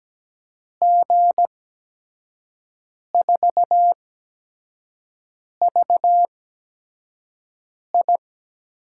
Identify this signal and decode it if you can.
Morse code, heard as a keyed tone, about 17 wpm, spaced as for 5 wpm, 700 Hz. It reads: G4VI
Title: USA Call Signs - 8x Character Spacing - 17wpm